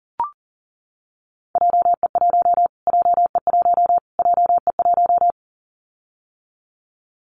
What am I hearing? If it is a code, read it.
Morse code, heard as a keyed tone, about 40 wpm, 700 Hz. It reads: JE1 JE1 JE1
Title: Callsign Prefixes for Top 25 DXCC - Repeated 3x - 40wpm